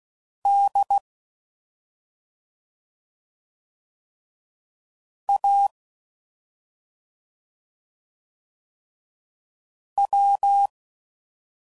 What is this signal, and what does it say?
Morse code, heard as a keyed tone, about 16 wpm, 800 Hz. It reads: D A W